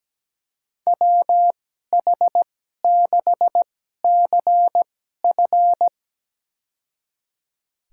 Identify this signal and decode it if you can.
Morse code, heard as a keyed tone, about 17 wpm, spaced as for 12 wpm, 700 Hz. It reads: WH6CF
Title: USA Call Signs - 2x Character Spacing - 17wpm